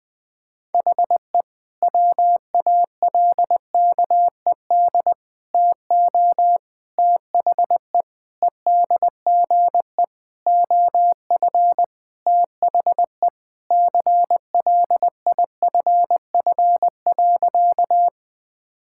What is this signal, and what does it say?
Morse code, heard as a keyed tone, about 20 wpm, 700 Hz. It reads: HE WALKED TO THE EDGE OF THE CLIFF.
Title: Sentences with 5-Word Phrases - 20wpm